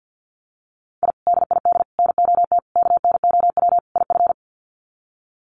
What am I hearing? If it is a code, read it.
Morse code, heard as a keyed tone, about 50 wpm, 700 Hz. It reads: I DID NOT KNOW IF
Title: Sets of 5 Words Encore 50wpm